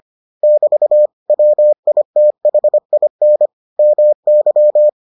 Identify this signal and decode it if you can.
Morse code, heard as a keyed tone, about 25 wpm, 600 Hz. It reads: <BT> WITHIN MY